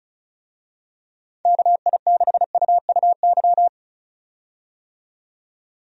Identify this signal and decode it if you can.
Morse code, heard as a keyed tone, about 34 wpm, 700 Hz. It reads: KI6UUY
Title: USA Call Signs - 6 Characters - 35wpm